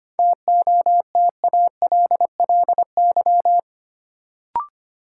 Morse code, heard as a keyed tone, about 25 wpm, 700 Hz. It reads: TOTALLY
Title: Sentences from Top 1600 Words - New Word Drill Alt - 25wpm